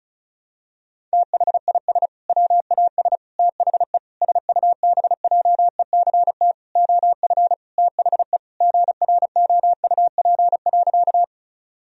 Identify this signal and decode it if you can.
Morse code, heard as a keyed tone, about 34 wpm, 700 Hz. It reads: THIS WAS THE SUBJECT OF THE GROUP.
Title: Sentences from Top 1000 Words - Review Part 1 - 35wpm